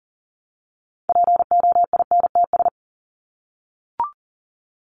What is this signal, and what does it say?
Morse code, heard as a keyed tone, about 40 wpm, 700 Hz. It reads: POINTS
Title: ES:S Suffix - Top 100 Words with Base Words 40wpm